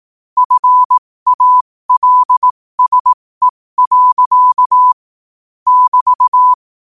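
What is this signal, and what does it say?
Morse code, heard as a keyed tone, about 18 wpm, spaced as for 15 wpm, 1000 Hz. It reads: FALSE. <BT>
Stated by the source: Morse code quote of the day sent at 15 words per minute.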